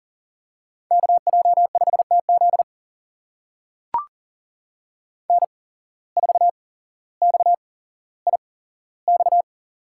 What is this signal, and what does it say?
Morse code, heard as a keyed tone, about 40 wpm, spaced as for 17 wpm, 700 Hz. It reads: KJ5TZ N4XIX
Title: USA Call Signs - 5 Characters - 8x Character Spacing - 40wpm